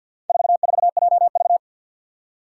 Morse code, heard as a keyed tone, about 46 wpm, 700 Hz. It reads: 44JV
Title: Four Challenging Letter-Number Combinations 50wpm